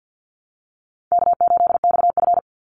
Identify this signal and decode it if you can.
Morse code, heard as a keyed tone, about 50 wpm, 700 Hz. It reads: K8XF
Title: Call Signs Repeated 3x - 50wpm